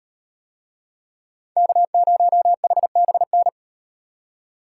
Morse code, heard as a keyed tone, about 36 wpm, 700 Hz. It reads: K0HBN